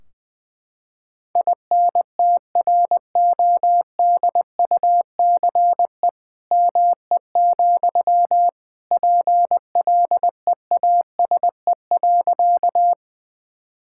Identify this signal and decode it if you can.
Morse code, heard as a keyed tone, about 21 wpm, 700 Hz. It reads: INTRODUCE ME, PLEASE.
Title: Sentences from Top 1600 Words 20wpm